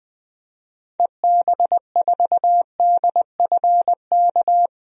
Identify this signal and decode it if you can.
Morse code, heard as a keyed tone, about 20 wpm, 700 Hz. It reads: EB4DFK
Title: DXCC-100 Call Signs - 6 Characters - 20wpm